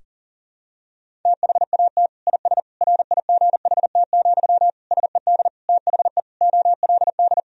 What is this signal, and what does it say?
Morse code, heard as a keyed tone, about 40 wpm, 700 Hz. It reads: THAT IS RIGHT, SED THE OLD
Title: Sentences from Top 200 Words and Abbreviations 40wpm